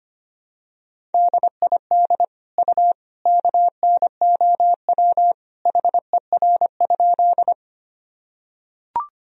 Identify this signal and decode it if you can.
Morse code, heard as a keyed tone, about 25 wpm, 700 Hz. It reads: DID U KNOW HER?
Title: Sentences from Top 100 Words and Abbreviations 25wpm